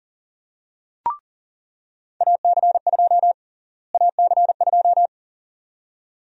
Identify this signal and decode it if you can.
Morse code, heard as a keyed tone, about 40 wpm, 700 Hz. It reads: AC2 AC2
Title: USA Call Sign Prefixes - Repeated 2x - 40wpm